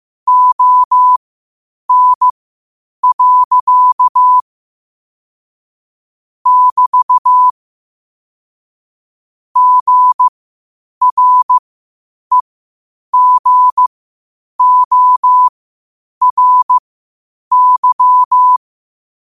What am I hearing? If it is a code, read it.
Morse code, heard as a keyed tone, about 15 wpm, spaced as for 8 wpm, 1000 Hz. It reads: ON. <BT> GREGORY
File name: qotdfiles / 8wpm